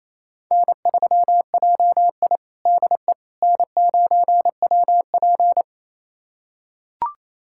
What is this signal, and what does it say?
Morse code, heard as a keyed tone, about 27 wpm, 700 Hz. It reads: N3JI DE N9WP
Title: USA Callsign DE Callsign - 4 Characters - 28wpm